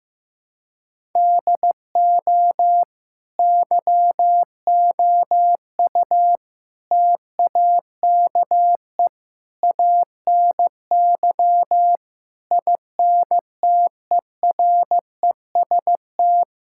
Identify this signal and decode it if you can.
Morse code, heard as a keyed tone, about 15 wpm, 700 Hz. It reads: DO YOU TAKE ANY INTEREST
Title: Sentences from Top 1000 Words - Review Part 6 - 15wpm